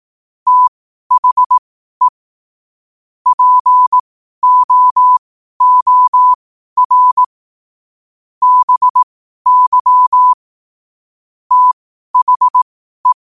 Morse code, heard as a keyed tone, about 18 wpm, spaced as for 12 wpm, 1000 Hz. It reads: THE POOR BY THE
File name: qotdfiles / 12wpm